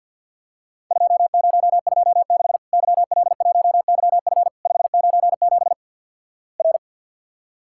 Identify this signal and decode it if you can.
Morse code, heard as a keyed tone, about 50 wpm, 700 Hz. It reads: 2026 YL1YF 597 R
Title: Worked All Europe DX Contest - QTC Exchange - 50wpm